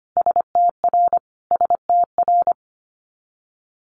Morse code, heard as a keyed tone, about 25 wpm, 700 Hz. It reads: STR STR
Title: Multi-letter Phonemes Repeated 3x 25wpm